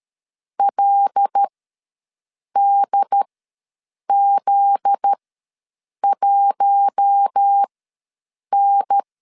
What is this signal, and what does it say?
Morse code, heard as a keyed tone, about 13 wpm, spaced as for 7 wpm, 800 Hz. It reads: LDZ1N
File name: LEZIONI / 9 2 ws 12wpm 3406